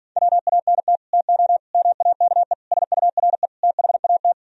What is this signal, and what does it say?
Morse code, heard as a keyed tone, about 57 wpm, 700 Hz. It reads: WANT TO MAKE SURE THAT